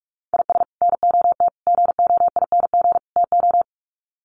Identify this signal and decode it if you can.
Morse code, heard as a keyed tone, about 44 wpm, 700 Hz. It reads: IS NOT GOING TO
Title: Sets of 4 Words 45wpm